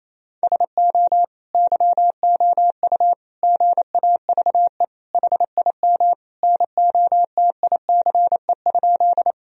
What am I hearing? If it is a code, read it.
Morse code, heard as a keyed tone, about 27 wpm, 700 Hz. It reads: SO YOU GAVE HIM NOTICE?
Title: Sentences from Top 1000 Words - Review Part 2 - 28wpm